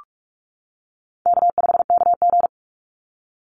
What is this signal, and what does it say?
Morse code, heard as a keyed tone, about 46 wpm, 700 Hz. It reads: K5KG
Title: Call Signs - 45wpm